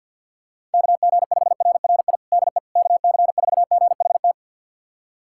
Morse code, heard as a keyed tone, about 53 wpm, 700 Hz. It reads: KG5RRI DE KC4GHT
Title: USA Callsign DE Callsign - 6 Characters - 50wpm